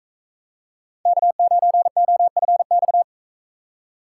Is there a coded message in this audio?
Yes: Morse code from 1.1 to 3.0 s, heard as a keyed tone, about 41 wpm, 700 Hz. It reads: K9OFX